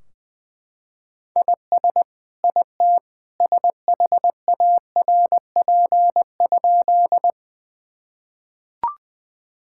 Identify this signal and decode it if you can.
Morse code, heard as a keyed tone, about 21 wpm, 700 Hz. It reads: IS IT SHARP?
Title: Sentences from Top 1600 Words 20wpm